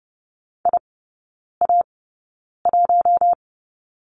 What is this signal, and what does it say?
Morse code, heard as a keyed tone, about 30 wpm, spaced as for 9 wpm, 700 Hz. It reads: IA1
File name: Lesson 18e - Letter m - Three Characters 30wpm